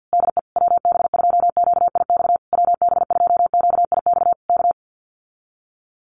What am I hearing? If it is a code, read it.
Morse code, heard as a keyed tone, about 48 wpm, 700 Hz. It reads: DE WB2QIX WB2QIX K
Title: Common QSO Phrases Encore 50wpm